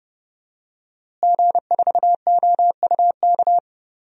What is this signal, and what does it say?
Morse code, heard as a keyed tone, about 30 wpm, 700 Hz. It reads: G4OUK
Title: DXCC-100 Call Signs - 5 Characters - Repeated 3x - 30wpm